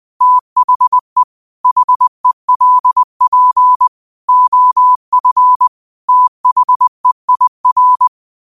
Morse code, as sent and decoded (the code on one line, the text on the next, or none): - .... / .... .-.. .--. / --- ..-. / - .... .. .-.
THE HELP OF THEIR